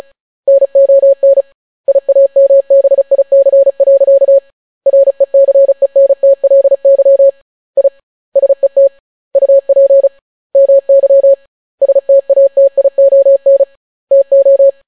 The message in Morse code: NON IAMBIC. RECENTLY I SET UP MY STATION TO

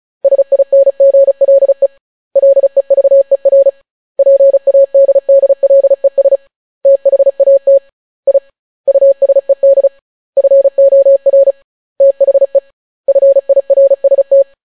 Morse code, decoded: SINGLE LEVER PADDLES THAT I USED FOR THE FIRST

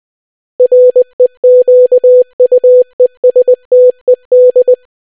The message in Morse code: REQUESTED

